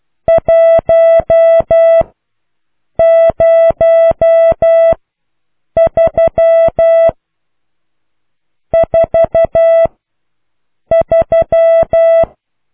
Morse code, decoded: 1 0 3 4 3